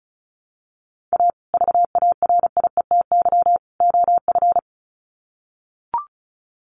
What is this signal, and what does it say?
Morse code, heard as a keyed tone, about 34 wpm, 700 Hz. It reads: A VARIETY OF
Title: Sets of 3 Words 35wpm